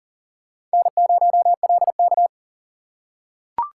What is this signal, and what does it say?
Morse code, heard as a keyed tone, about 40 wpm, 700 Hz. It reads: N0LK
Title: USA Call Signs - 4x Character Spacing - 40wpm